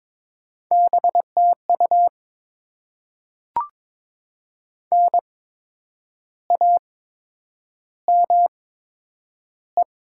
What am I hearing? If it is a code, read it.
Morse code, heard as a keyed tone, about 22 wpm, spaced as for 9 wpm, 700 Hz. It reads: BTU NAME
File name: Top 20 QSO Elements - 8x Character Spacing - 22wpm